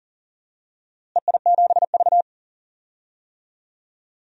Morse code, heard as a keyed tone, about 39 wpm, 690 Hz. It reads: EI7V
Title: DXCC-100 Call Signs - 4 Characters - Repeated 2x - 40wpm